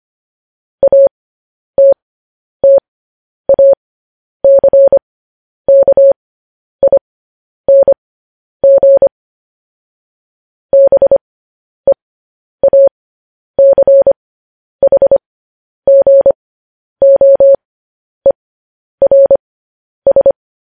Morse code, heard as a keyed tone, about 25 wpm, spaced as for 10 wpm, 550 Hz. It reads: ATTACKING BEACHGOERS
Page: news headlines in morse code at 10 wpm